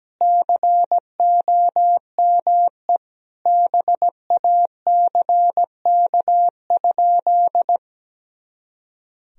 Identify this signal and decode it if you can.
Morse code, heard as a keyed tone, about 17 wpm, 700 Hz. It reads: COME BACK?
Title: Sentences from Top 300 Words and Abbreviations 17wpm